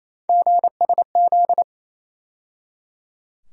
Morse code, heard as a keyed tone, about 28 wpm, 700 Hz. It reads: GSZ
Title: USA Call Sign Postfixes - 3 Letters 28wpm